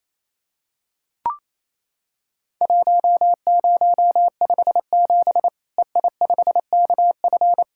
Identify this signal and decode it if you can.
Morse code, heard as a keyed tone, about 27 wpm, 700 Hz. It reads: 1057 EI5KF
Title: Worked All Europe DX Contest - QTC Exchange - 28wpm